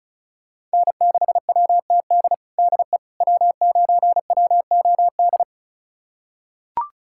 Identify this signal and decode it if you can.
Morse code, heard as a keyed tone, about 34 wpm, 690 Hz. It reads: N6WTD DE W9WOD
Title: USA Callsign DE Callsign - 5 Characters - 35wpm